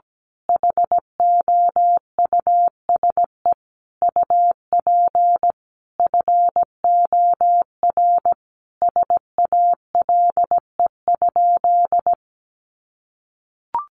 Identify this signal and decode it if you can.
Morse code, heard as a keyed tone, about 17 wpm, 700 Hz. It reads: HOUSE UP FOR SALE?
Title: Sentences from Top 1400 Words 17wpm